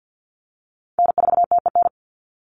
Morse code, heard as a keyed tone, about 48 wpm, 700 Hz. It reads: N4TEN
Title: USA Call Signs 50wpm